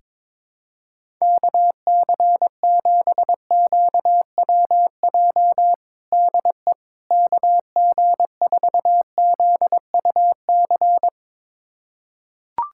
Morse code, heard as a keyed tone, about 21 wpm, 700 Hz. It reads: KC7QWJ DE KG4ZUC
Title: USA Callsign DE Callsign - 6 Characters - 22wpm